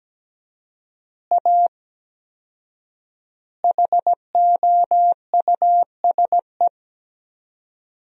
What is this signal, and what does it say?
Morse code, heard as a keyed tone, about 17 wpm, 700 Hz. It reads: A HOUSE
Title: 4x Word-Spacing - Sentences from Top 200 Words - 17wpm